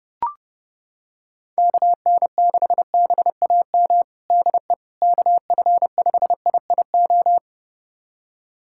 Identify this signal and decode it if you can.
Morse code, heard as a keyed tone, about 30 wpm, 700 Hz. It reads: KN6BAM DE KF5IIO
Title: USA Callsign DE Callsign - 6 Characters - 30wpm